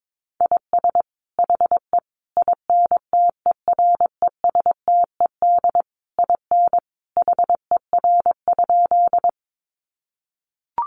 Morse code, heard as a keyed tone, about 22 wpm, 700 Hz. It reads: IS HE INTERESTED IN HER?
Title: Sentences from Top 900 Words 22wpm